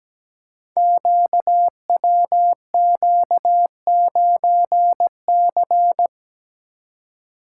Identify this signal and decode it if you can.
Morse code, heard as a keyed tone, about 17 wpm, 700 Hz. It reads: QWQ9C
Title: Code Groups - 5 Letter:Numbers - 17wpm